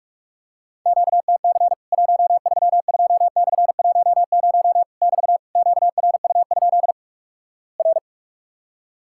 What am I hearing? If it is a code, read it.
Morse code, heard as a keyed tone, about 48 wpm, 700 Hz. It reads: QTC 132/10 <BT> QRU? R
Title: Worked All Europe DX Contest - QTC Exchange - 45wpm